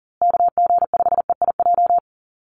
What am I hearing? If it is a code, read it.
Morse code, heard as a keyed tone, about 40 wpm, 700 Hz. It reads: KG5EIJ
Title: USA Call Signs Repeated 3x 40wpm